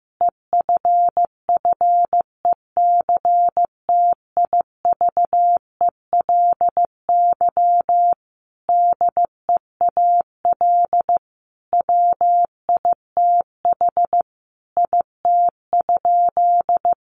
Morse code, heard as a keyed tone, about 15 wpm, 700 Hz. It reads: EFFECTIVELY DEAL WITH IT?
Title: Sentences from Top 1400 Words 15wpm